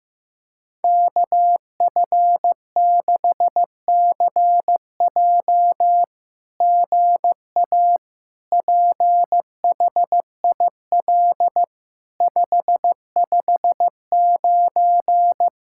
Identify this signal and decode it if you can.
Morse code, heard as a keyed tone, about 15 wpm, 700 Hz. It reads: KF6CJ GA PHIL 559
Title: SOTA QSO Practice 15wpm